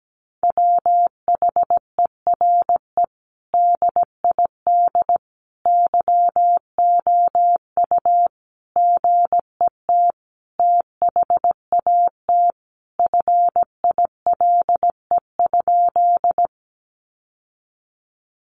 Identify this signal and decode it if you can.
Morse code, heard as a keyed tone, about 17 wpm, 700 Hz. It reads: WHERE DID YOU GET THAT FILE?
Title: Sentences from Top 1200 Words 17wpm